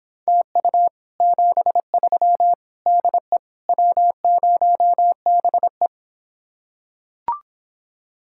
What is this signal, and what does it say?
Morse code, heard as a keyed tone, about 26 wpm, 700 Hz. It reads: TU 73 DE W0BE